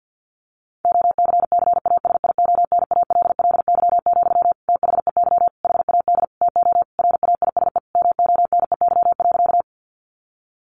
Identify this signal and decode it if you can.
Morse code, heard as a keyed tone, about 50 wpm, 700 Hz. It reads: OCCASIONALLY, THEY HAD TO RAISE MONEY.
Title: Sentences from Top 1400 Words 50wpm